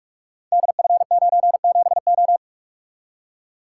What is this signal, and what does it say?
Morse code, heard as a keyed tone, about 44 wpm, 700 Hz. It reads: DF97O
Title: Code Groups - 5 Letter:Numbers - 45wpm